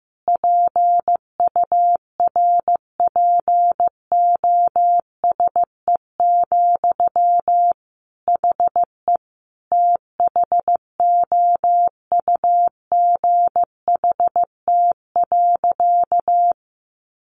Morse code, heard as a keyed tone, about 15 wpm, 700 Hz. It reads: PURPOSE, HE THOUGHT.
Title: Sentences from Top 800 Words 15wpm